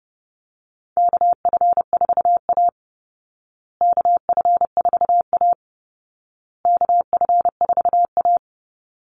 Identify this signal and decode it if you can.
Morse code, heard as a keyed tone, about 30 wpm, 700 Hz. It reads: KF4A KF4A KF4A
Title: Call Signs Repeated 2x - 30wpm